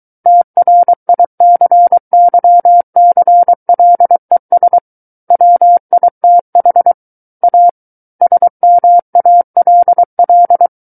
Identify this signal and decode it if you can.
Morse code, heard as a keyed tone, about 23 wpm, 700 Hz. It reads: TRICYCLES WITH A SMALL
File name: Book One / 14-Chapter 14 - 23 WPM